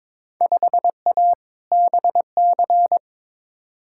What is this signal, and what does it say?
Morse code, heard as a keyed tone, about 22 wpm, 700 Hz. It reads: 5A BC
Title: Field Day Exchanges - 22wpm